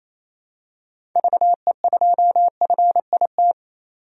Morse code, heard as a keyed tone, about 28 wpm, 700 Hz. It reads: VE2FIT
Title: DXCC-100 Call Signs - 6 Characters - 28wpm